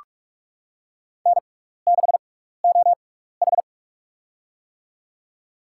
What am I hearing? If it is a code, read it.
Morse code, heard as a keyed tone, about 43 wpm, spaced as for 15 wpm, 700 Hz. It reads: N6OH
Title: USA Call Signs - 4 Characters - 6x Character Spacing - 45wpm